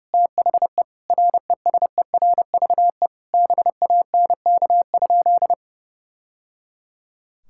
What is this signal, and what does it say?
Morse code, heard as a keyed tone, about 30 wpm, 700 Hz. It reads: THE RESERVE BANK?
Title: Sentences from Top 1000 Words - Review Part 6 - 30wpm